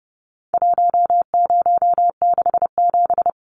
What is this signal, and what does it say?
Morse code, heard as a keyed tone, about 30 wpm, 700 Hz. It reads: 1067
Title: Numbers - 4-Digit 30wpm